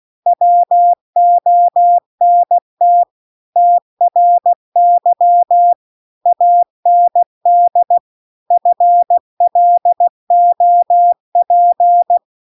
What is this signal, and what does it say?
Morse code, heard as a keyed tone, about 16 wpm, 700 Hz. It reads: WONT TRY AND FLOP
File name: CW-The-Big-Twoheaded-River-2-16WPM